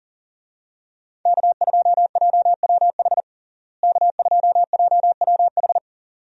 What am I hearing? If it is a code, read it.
Morse code, heard as a keyed tone, about 40 wpm, 700 Hz. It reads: K2JWH K2JWH
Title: USA Call Signs - 5 Characters - Repeated 2x - 40wpm